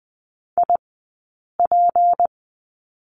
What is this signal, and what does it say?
Morse code, heard as a keyed tone, about 20 wpm, 700 Hz. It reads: I P
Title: 190 Lesson 31d - Letter v - Two Characters 20wpm